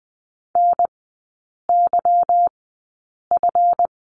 Morse code, heard as a keyed tone, about 20 wpm, spaced as for 8 wpm, 690 Hz. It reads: NYF
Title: 145 Lesson 24e - Letter y - Three Characters 20wpm